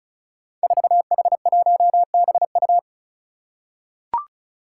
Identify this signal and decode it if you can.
Morse code, heard as a keyed tone, about 36 wpm, 700 Hz. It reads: 4H1BU
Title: Five Challenging Letter-Number Combinations 35wpm